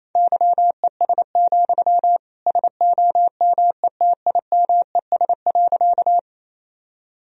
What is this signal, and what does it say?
Morse code, heard as a keyed tone, about 27 wpm, 700 Hz. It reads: YES, SOMETIMES.
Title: Sentences from Top 1000 Words - Review Part 3 - 28wpm